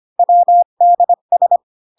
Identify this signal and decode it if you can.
Morse code, heard as a keyed tone, about 25 wpm, 700 Hz. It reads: WDS